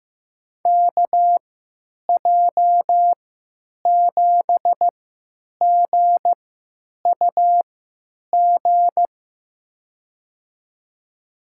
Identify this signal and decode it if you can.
Morse code, heard as a keyed tone, about 15 wpm, spaced as for 9 wpm, 700 Hz. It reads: KJ7GUG
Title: USA Call Signs - 6 Characters - 3x Character Spacing - 15wpm